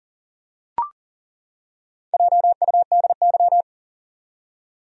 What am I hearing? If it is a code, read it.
Morse code, heard as a keyed tone, about 39 wpm, 700 Hz. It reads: JUDY
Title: USA Common Names - Max 5 Letters - 40wpm